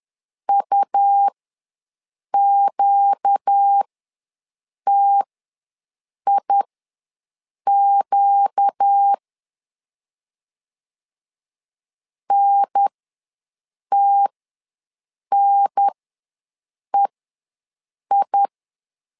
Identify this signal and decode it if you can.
Morse code, heard as a keyed tone, about 10 wpm, spaced as for 6 wpm, 800 Hz. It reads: UQTIQ NTNEI
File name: LEZIONI / 4 Q ws 10wpm 2306